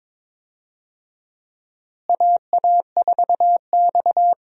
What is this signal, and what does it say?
Morse code, heard as a keyed tone, about 22 wpm, 700 Hz. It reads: AA4X
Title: USA Call Signs - 4 Characters - Repeated 3x - 22wpm